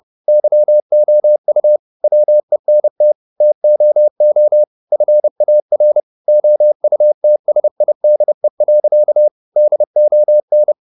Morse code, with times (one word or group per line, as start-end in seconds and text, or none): 0.3-1.8 YOU
2.0-3.1 WENT
3.4-4.6 TOO
4.9-6.0 FAR
6.3-9.3 OUTSIDE.
9.6-10.7 DON